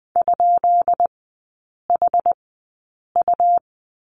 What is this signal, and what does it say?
Morse code, heard as a keyed tone, about 20 wpm, spaced as for 8 wpm, 690 Hz. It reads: ?HU
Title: 133 Lesson 22e - Question Mark - Three Characters 20wpm